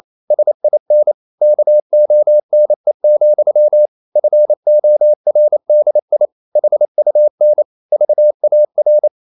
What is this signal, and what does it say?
Morse code, heard as a keyed tone, about 28 wpm, 600 Hz. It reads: SIN KONE, FORDI HUN VAR
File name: Ferie QRQ 28wpm